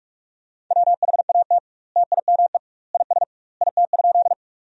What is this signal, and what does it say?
Morse code, heard as a keyed tone, about 43 wpm, 700 Hz. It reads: WHAT TIME IS IT?
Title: 2x Word-Spacing - Sentences from Top 100 Words - 45wpm